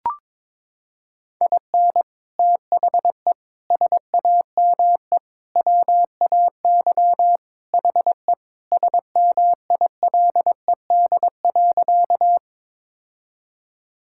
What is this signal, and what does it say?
Morse code, heard as a keyed tone, about 21 wpm, 700 Hz. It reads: IN THE SAME WAY HE SMILED.
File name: practice / 22wpm / Sentences with 4-Word Phrases - 22wpm